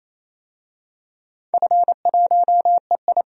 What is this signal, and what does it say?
Morse code, heard as a keyed tone, about 28 wpm, 700 Hz. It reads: F1EI
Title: DXCC-100 Call Signs - 4 Characters - 28wpm